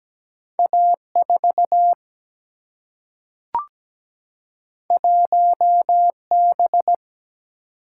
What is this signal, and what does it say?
Morse code, heard as a keyed tone, about 17 wpm, 700 Hz. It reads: A4 1B
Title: Single-Letter Single-Number 17wpm